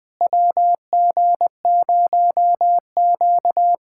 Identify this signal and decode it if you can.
Morse code, heard as a keyed tone, about 21 wpm, 700 Hz. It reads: WG0Q
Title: USA Callsign DE Callsign - 4 Characters - 20wpm